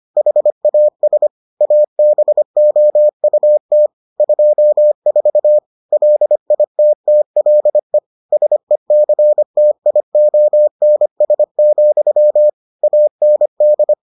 Morse code, heard as a keyed tone, about 25 wpm, 600 Hz. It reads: HAS ABOUT 24 LITTLE SECTIONS, AND